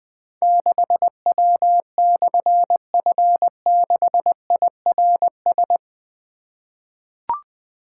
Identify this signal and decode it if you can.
Morse code, heard as a keyed tone, about 20 wpm, 700 Hz. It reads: 6W/F6IRS
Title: Difficult Call Signs 20wpm